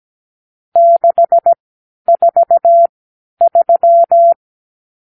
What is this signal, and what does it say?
Morse code, heard as a keyed tone, about 17 wpm, spaced as for 11 wpm, 680 Hz. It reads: 643
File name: k7qo / Lesson 50 (1-6)